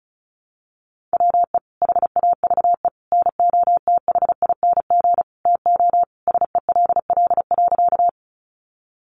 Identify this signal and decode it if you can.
Morse code, heard as a keyed tone, about 34 wpm, 700 Hz. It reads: WE HAVE NOTHING TO SELL.
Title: Sentences from Top 1100 Words 35wpm